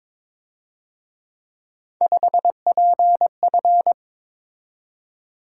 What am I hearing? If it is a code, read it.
Morse code, heard as a keyed tone, about 27 wpm, 700 Hz. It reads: 5PF
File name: Speed-Racing - Three Letter-Number Combinations - 22wpm